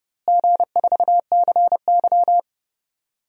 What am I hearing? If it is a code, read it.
Morse code, heard as a keyed tone, about 31 wpm, 700 Hz. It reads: G4CY
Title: DXCC-100 Call Signs - 4 Characters - Repeated 2x - 30wpm